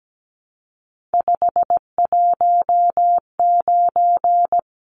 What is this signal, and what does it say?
Morse code, heard as a keyed tone, about 17 wpm, 700 Hz. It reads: 519
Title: Numbers - 3-Digit 17wpm